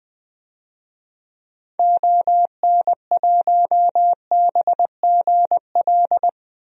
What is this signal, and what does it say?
Morse code, heard as a keyed tone, about 21 wpm, 700 Hz. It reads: ON1BGL
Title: DXCC-100 Call Signs - 6 Characters - Repeated 2x - 20wpm